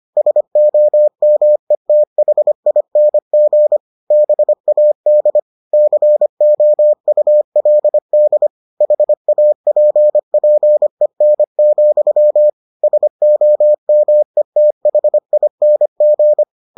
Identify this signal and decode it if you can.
Morse code, heard as a keyed tone, about 25 wpm, 600 Hz. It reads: SOMETHING BAD COULD HAPPEN, SOMETHING